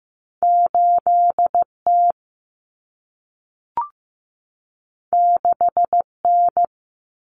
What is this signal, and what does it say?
Morse code, heard as a keyed tone, about 15 wpm, 700 Hz. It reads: 8T 6N
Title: Common Cut Numbers - 2-Digit 15wpm